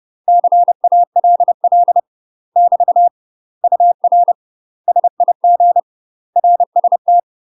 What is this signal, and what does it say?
Morse code, heard as a keyed tone, about 30 wpm, 700 Hz. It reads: CALL <BT> UR SIG RST
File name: UK MASTER QSOs TRAINING / 30WPM / UK MASTER QSOs 0001